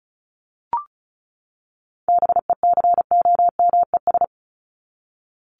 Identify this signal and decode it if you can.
Morse code, heard as a keyed tone, about 34 wpm, 700 Hz. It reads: BECOMES
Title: ES:S Suffix - Top 100 Words with Base Words 35wpm